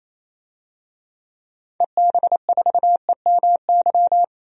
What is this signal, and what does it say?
Morse code, heard as a keyed tone, about 28 wpm, 700 Hz. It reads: EB4EMY
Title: DXCC-100 Call Signs - 6 Characters - 28wpm